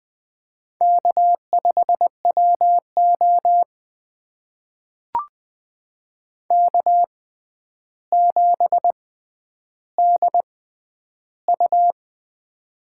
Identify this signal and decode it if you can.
Morse code, heard as a keyed tone, about 20 wpm, spaced as for 10 wpm, 700 Hz. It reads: K5WO K7DU
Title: USA Call Signs - 4 Characters - 6x Character Spacing - 20wpm